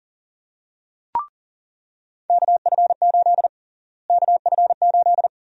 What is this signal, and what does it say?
Morse code, heard as a keyed tone, about 40 wpm, 700 Hz. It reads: KF8 KF8
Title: USA Call Sign Prefixes - Repeated 2x - 40wpm